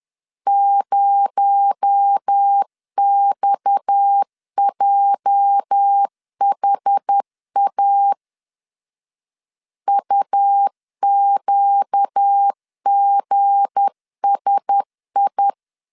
Morse code, heard as a keyed tone, about 10 wpm, 800 Hz. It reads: 0XJHA UQGSI
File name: LEZIONI / 8 YJ45 / 8 YJ45 10wpm 2549